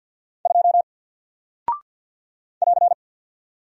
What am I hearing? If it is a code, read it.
Morse code, heard as a keyed tone, about 48 wpm, 700 Hz. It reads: 2 <AR>
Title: Single Character (Letter:Number:Punctuation:Prosign) Alt 50wpm